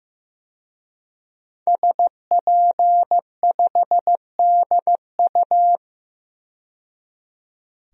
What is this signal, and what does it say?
Morse code, heard as a keyed tone, about 15 wpm, 700 Hz. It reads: SP5DU
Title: DXCC-100 Call Signs - 5 Characters - 15wpm